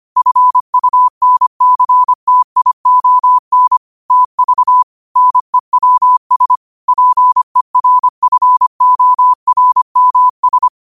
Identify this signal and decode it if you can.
Morse code, heard as a keyed tone, about 25 wpm, 1000 Hz. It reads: FUNCTION TV NEWS PERFORMS
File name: qotdfiles / 25wpm